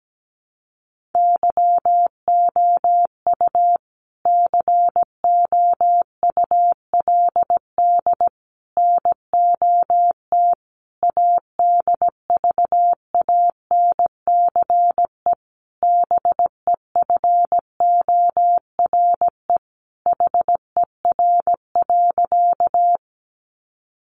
Morse code, heard as a keyed tone, about 17 wpm, 700 Hz. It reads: YOU COULD NOT ADVANCE BEFORE HER.
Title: Sentences from Top 1400 Words 17wpm